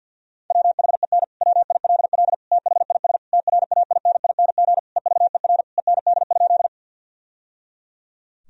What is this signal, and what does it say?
Morse code, heard as a keyed tone, about 50 wpm, 700 Hz. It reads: WHEN WILL THIS TRAINING EVER END?
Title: Sentences from Top 1000 Words - Review Part 4 - 50wpm